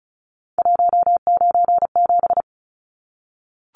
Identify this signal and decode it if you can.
Morse code, heard as a keyed tone, about 34 wpm, 700 Hz. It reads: 197
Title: Numbers - 3-Digit 35wpm